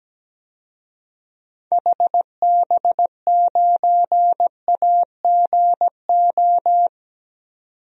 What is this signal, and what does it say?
Morse code, heard as a keyed tone, about 17 wpm, 700 Hz. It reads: HB9AGO
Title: DXCC-100 Call Signs - 6 Characters - 17wpm